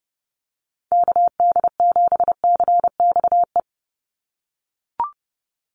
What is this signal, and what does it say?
Morse code, heard as a keyed tone, about 30 wpm, 700 Hz. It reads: KD7CXE
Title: USA Call Signs 30wpm